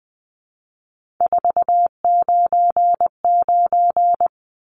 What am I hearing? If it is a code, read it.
Morse code, heard as a keyed tone, about 20 wpm, 700 Hz. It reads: 499
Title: Numbers - 3-Digit 20wpm